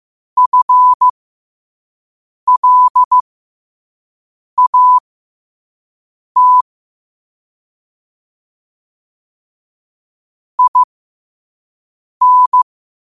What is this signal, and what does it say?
Morse code, heard as a keyed tone, about 15 wpm, spaced as for 5 wpm, 1000 Hz. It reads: FLAT IN